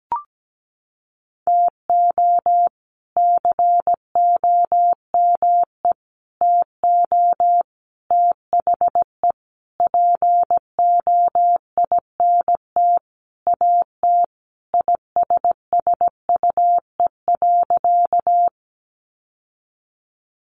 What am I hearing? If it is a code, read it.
Morse code, heard as a keyed tone, about 17 wpm, 700 Hz. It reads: TO COME TO THE POINT AT ISSUE.
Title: Sentences from Top 900 Words 17wpm